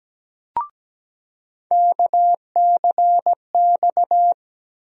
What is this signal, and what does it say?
Morse code, heard as a keyed tone, about 17 wpm, 700 Hz. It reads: KCX
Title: USA Call Sign Postfixes - 3 Letters 17wpm